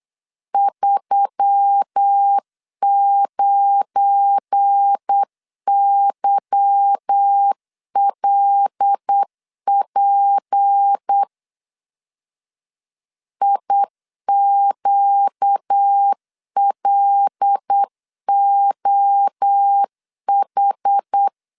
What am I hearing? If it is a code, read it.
Morse code, heard as a keyed tone, about 8 wpm, 800 Hz. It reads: 39YLP IQLOH
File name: LEZIONI / 8 YJ45 / 8 Y 8wpm 3245